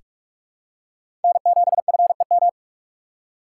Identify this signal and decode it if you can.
Morse code, heard as a keyed tone, about 47 wpm, 700 Hz. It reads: N7FEM
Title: USA Call Signs - 5 Characters - 4x Character Spacing - 45wpm